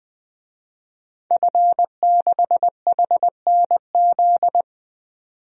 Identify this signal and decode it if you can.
Morse code, heard as a keyed tone, about 20 wpm, 700 Hz. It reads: F6HNZ
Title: DXCC-100 Call Signs - 5 Characters - 20wpm